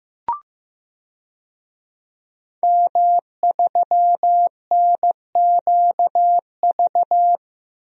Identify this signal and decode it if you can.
Morse code, heard as a keyed tone, about 15 wpm, 700 Hz. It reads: M3NQV
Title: DXCC-100 Call Signs - 5 Characters - 15wpm